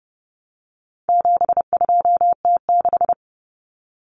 Morse code, heard as a keyed tone, about 30 wpm, 700 Hz. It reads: 72T6
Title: Common Cut Numbers - 4-Digit 30wpm